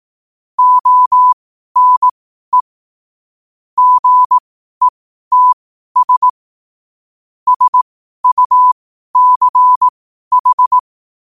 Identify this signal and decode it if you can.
Morse code, heard as a keyed tone, about 18 wpm, spaced as for 12 wpm, 1000 Hz. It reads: ONE GETS SUCH